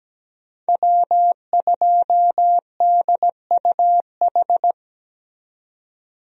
Morse code, heard as a keyed tone, about 17 wpm, 700 Hz. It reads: W2DUH
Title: USA Call Signs - 5 Characters - 17wpm